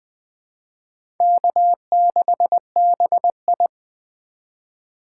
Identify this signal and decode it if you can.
Morse code, heard as a keyed tone, about 20 wpm, 700 Hz. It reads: K6BI
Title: USA Call Signs - 4 Characters - 3x Character Spacing - 20wpm